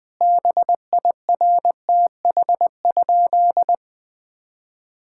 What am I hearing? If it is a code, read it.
Morse code, heard as a keyed tone, about 20 wpm, 700 Hz. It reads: BIRTH?
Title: Sentences from Top 1000 Words - Review Part 6 - 20wpm